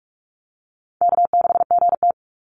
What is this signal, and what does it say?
Morse code, heard as a keyed tone, about 44 wpm, 700 Hz. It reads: K6GT
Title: Call Signs Repeated 3x - 45wpm